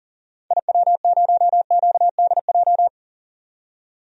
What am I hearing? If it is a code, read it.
Morse code, heard as a keyed tone, about 37 wpm, 700 Hz. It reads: IW0QDJ
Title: DXCC-100 Call Signs - 6 Characters - 40wpm